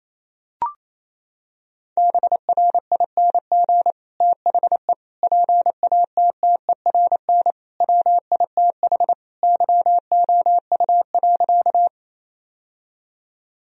Morse code, heard as a keyed tone, about 28 wpm, 700 Hz. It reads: BRING THE PATTERN WITH YOU.
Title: Sentences from Top 1000 Words - Review Part 2 - 28wpm